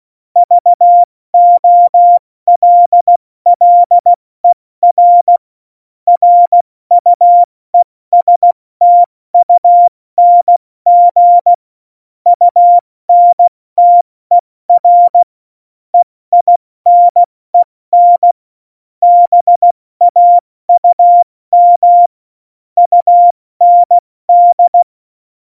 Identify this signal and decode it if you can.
Morse code, heard as a keyed tone, about 16 wpm, spaced as for 14 wpm, 700 Hz. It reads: VOLLER RUESTUNG UNTER EINEN BAUM UND
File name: Morsekurs / Witze / Witz 15. Mai 2018, 70 F80